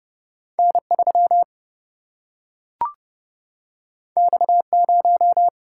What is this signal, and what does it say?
Morse code, heard as a keyed tone, about 30 wpm, 700 Hz. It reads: N 3 X 0
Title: Single-Letter Single-Number 30wpm